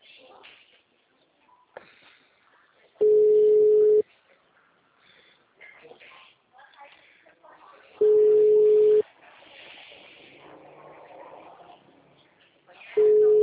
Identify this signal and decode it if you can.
Ringback tone (Europe, 425 Hz), with speech.